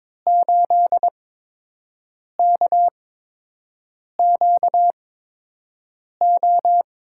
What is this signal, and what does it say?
Morse code, heard as a keyed tone, about 22 wpm, spaced as for 6 wpm, 700 Hz. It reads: 8KQO
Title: USA Call Signs - 6 Characters - 8x Character Spacing - 22wpm